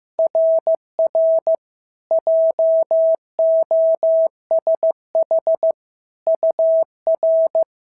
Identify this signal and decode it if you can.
Morse code, heard as a keyed tone, about 15 wpm, 650 Hz. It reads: RR JOSH UR
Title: SOTA QSO Practice 15wpm